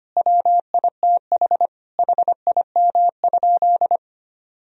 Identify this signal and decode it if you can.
Morse code, heard as a keyed tone, about 25 wpm, 700 Hz. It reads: WITH HIM?
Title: Sentences from Top 1000 Words - Review Part 3 - 25wpm